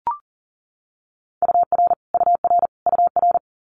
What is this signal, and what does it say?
Morse code, heard as a keyed tone, about 37 wpm, 700 Hz. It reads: UR UR UR
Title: Multi-letter Phonemes Repeated 3x 40wpm